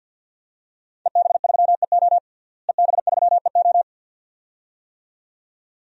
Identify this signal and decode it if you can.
Morse code, heard as a keyed tone, about 47 wpm, 700 Hz. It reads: EB3EO EB3EO
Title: DXCC-100 Call Signs - 5 Characters - Repeated 2x - 50wpm